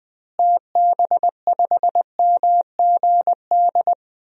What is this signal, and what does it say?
Morse code, heard as a keyed tone, about 21 wpm, 700 Hz. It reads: TB5MGD
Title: DXCC-100 Call Signs - 6 Characters - Repeated 2x - 20wpm